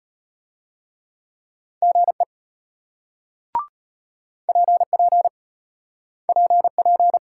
Morse code, heard as a keyed tone, about 34 wpm, 700 Hz. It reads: GE PP PP